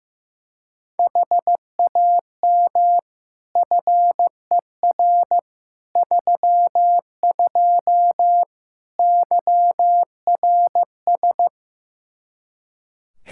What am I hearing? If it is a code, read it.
Morse code, heard as a keyed tone, about 16 wpm, 700 Hz. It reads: HAM FER 32 YRS
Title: Practice QSO - Exchange 3 - More Info - 15wpm